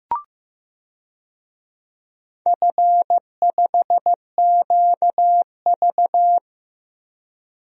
Morse code, heard as a keyed tone, about 15 wpm, 700 Hz. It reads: F5QV
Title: DXCC-100 Call Signs - 4 Characters - 15wpm